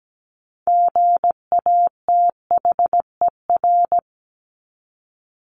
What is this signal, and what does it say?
Morse code, heard as a keyed tone, about 17 wpm, 700 Hz. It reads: GATHER
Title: Top 1501-1750 Words 17wpm